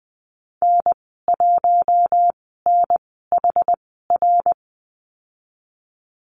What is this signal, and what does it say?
Morse code, heard as a keyed tone, about 20 wpm, spaced as for 15 wpm, 700 Hz. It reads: N1NHR
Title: 062 Lesson 11g - Letter h - Call Signs 20wpm